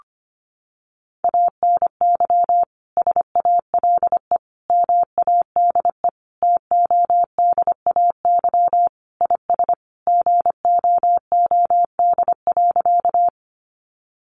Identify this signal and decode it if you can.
Morse code, heard as a keyed tone, about 25 wpm, 700 Hz. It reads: ANY SALE MADE TODAY IS GOOD.
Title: Sentences from Top 1400 Words 25wpm